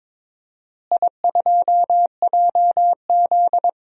Morse code, heard as a keyed tone, about 22 wpm, 700 Hz. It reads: I2JZ
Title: DXCC-100 Call Signs - 4 Characters - Repeated 3x - 22wpm